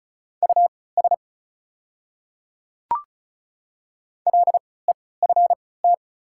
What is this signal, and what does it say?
Morse code, heard as a keyed tone, about 36 wpm, spaced as for 20 wpm, 690 Hz. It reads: US LEFT
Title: 3x Character Spacing - Top 500 Words - 35wpm